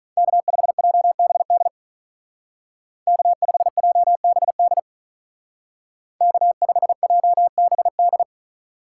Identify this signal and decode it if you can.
Morse code, heard as a keyed tone, about 41 wpm, 700 Hz. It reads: K5JBD K5JBD K5JBD
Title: Speed Racing - USA Call Signs 35wpm